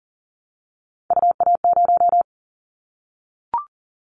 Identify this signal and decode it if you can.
Morse code, heard as a keyed tone, about 40 wpm, 700 Hz. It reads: UA0
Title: Callsign Prefixes for Top 100 DXCC - 40wpm